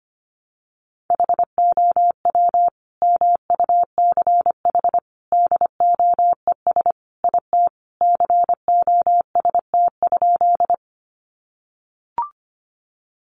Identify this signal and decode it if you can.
Morse code, heard as a keyed tone, about 25 wpm, 700 Hz. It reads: HOW MUCH DOES IT COST?
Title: Sentences from Top 800 Words 25wpm